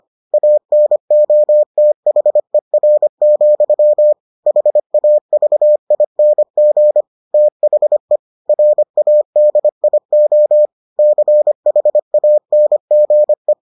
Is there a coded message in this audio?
Yes: Morse code from 0.3 to 13.5 s, heard as a keyed tone, about 25 wpm, 600 Hz. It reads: ANOTHER, HAVING THE RADIO CHANGE